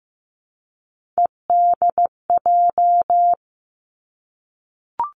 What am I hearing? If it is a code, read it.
Morse code, heard as a keyed tone, about 15 wpm, 700 Hz. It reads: EDJ
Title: Three Challenging Letter-Number Combinations 15wpm